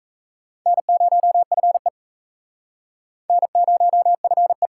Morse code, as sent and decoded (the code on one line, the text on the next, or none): -. ----- ..-. / -. ----- ..-. .
N0FE N0FE